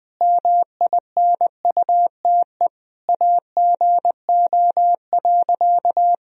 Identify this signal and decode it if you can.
Morse code, heard as a keyed tone, about 20 wpm, 700 Hz. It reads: MINUTE AGO.
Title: Sentences from Top 500 Words and Abbreviations 20wpm